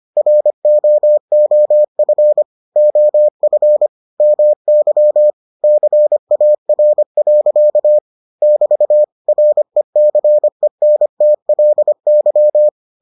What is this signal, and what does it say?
Morse code, heard as a keyed tone, about 25 wpm, 600 Hz. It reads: ROOF OF MY CAR. <BT> RECENTLY